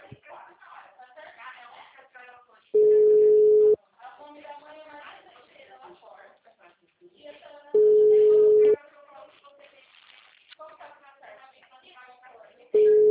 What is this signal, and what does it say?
Ringback tone (Europe, 425 Hz), with speech.
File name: out- -2010- - - .24108